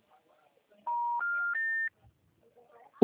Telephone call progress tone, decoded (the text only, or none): special information tone